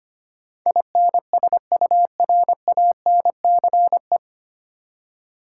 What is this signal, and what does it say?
Morse code, heard as a keyed tone, about 25 wpm, 700 Hz. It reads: INSURANCE
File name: Nine Letter Words - 25wpm